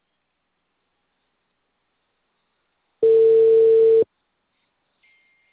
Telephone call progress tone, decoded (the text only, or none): ringback tone